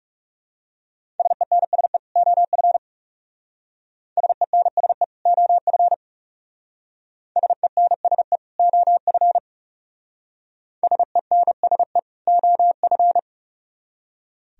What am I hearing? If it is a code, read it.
Morse code, heard as a keyed tone, about 35 wpm, 700 Hz. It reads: SENSE OF SENSE OF SENSE OF SENSE OF
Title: Speed-Racing - Sets of 2 Words 30wpm